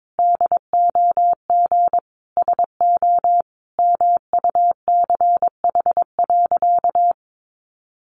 Morse code, heard as a keyed tone, about 22 wpm, 700 Hz. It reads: DOG SO MUCH.
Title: Additional Easy Phrases 22wpm